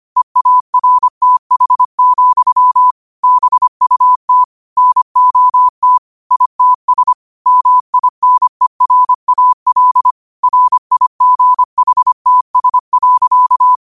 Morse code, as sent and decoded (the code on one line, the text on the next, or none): . .- .-. - .... --..-- / -... ..- - / -. --- - / .. - ... / -- .. -. .-. .- .-.. / .-. .. --. .... - ... .-.-.-
EARTH, BUT NOT ITS MINERAL RIGHTS.